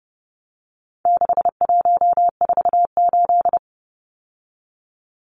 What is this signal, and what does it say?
Morse code, heard as a keyed tone, about 30 wpm, 700 Hz. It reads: 6148
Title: Numbers - 4-Digit 30wpm